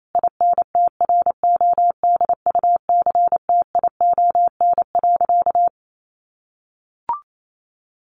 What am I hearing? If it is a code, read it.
Morse code, heard as a keyed tone, about 28 wpm, 700 Hz. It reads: INTRODUCTION.
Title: Sentences from Top 1000 Words 28wpm